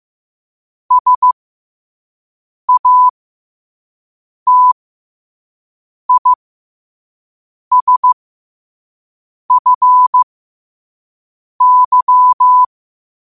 Morse code, heard as a keyed tone, about 15 wpm, spaced as for 5 wpm, 1000 Hz. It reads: SATISFY
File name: qotdfiles / 5wpm